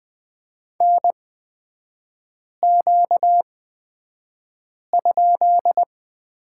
Morse code, heard as a keyed tone, about 20 wpm, spaced as for 5 wpm, 700 Hz. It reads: NQ?
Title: 236 Lesson 38e - Letter q - Three Characters 20wpm